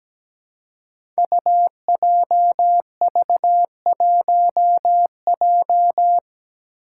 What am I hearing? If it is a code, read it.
Morse code, heard as a keyed tone, about 17 wpm, 700 Hz. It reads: UJV1J
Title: Five Challenging Letter-Number Combinations 17wpm